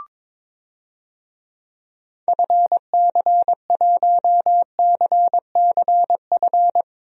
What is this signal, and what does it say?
Morse code, heard as a keyed tone, about 22 wpm, 700 Hz. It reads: FC1CCF
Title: DXCC-100 Call Signs - 6 Characters - 22wpm